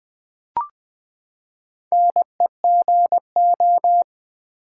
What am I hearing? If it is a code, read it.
Morse code, heard as a keyed tone, about 20 wpm, 700 Hz. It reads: NEGO
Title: Build A Word - 4 Letters At A Time - 11-Letter Words - 20wpm